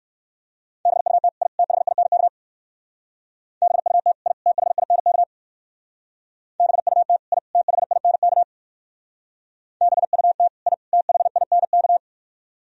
Morse code, heard as a keyed tone, about 63 wpm, 700 Hz. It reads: BUT I THINK BUT I THINK BUT I THINK BUT I THINK